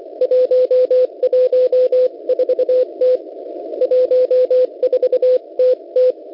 Morse code, heard as a keyed tone, about 24 wpm, 510 Hz. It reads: 114T 14TT
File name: Unid- -0700utc-4735khz